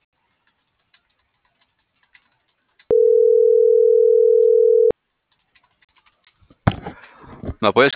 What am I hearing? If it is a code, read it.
Ringback tone (North America), 440 + 480 Hz.